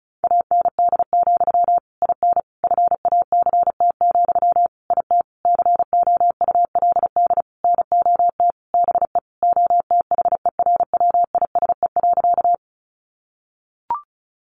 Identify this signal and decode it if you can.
Morse code, heard as a keyed tone, about 34 wpm, 700 Hz. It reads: AND, IN FACT, IT COULD NOT BE OTHERWISE.
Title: Sentences from Top 1200 Words 35wpm